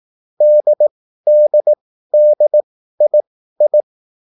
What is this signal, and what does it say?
Morse code, heard as a keyed tone, about 18 wpm, spaced as for 13 wpm, 600 Hz. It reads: DDDII
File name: QSL / CW lesson